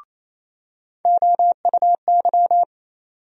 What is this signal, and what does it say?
Morse code, heard as a keyed tone, about 28 wpm, 700 Hz. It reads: OUY
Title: USA Call Sign Postfixes - 3 Letters 28wpm